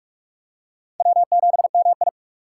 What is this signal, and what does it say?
Morse code, heard as a keyed tone, about 40 wpm, 700 Hz. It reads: W7MI